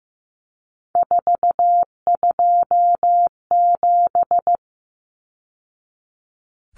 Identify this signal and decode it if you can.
Morse code, heard as a keyed tone, about 15 wpm, 700 Hz. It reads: 427
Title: Numbers - 3-Digit 15wpm